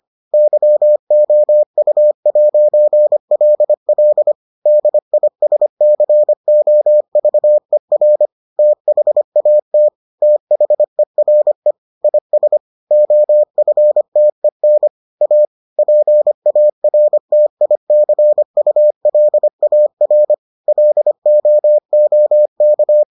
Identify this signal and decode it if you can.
Morse code, heard as a keyed tone, about 25 wpm, 600 Hz. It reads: YOU'LL DISCOVER THAT THERE IS OFTEN A PARTICULAR LOOK